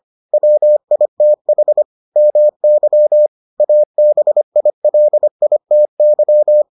Morse code, heard as a keyed tone, about 25 wpm, 600 Hz. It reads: WITH MY ABILITY